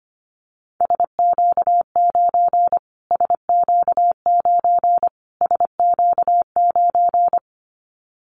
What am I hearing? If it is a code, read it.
Morse code, heard as a keyed tone, about 25 wpm, 700 Hz. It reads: SQ9 SQ9 SQ9
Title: Callsign Prefixes for Top 25 DXCC - Repeated 3x - 25wpm